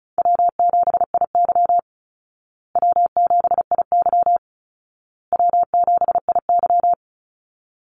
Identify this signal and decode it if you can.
Morse code, heard as a keyed tone, about 34 wpm, 700 Hz. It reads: W7IY W7IY W7IY
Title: Call Signs Repeated 2x - 35wpm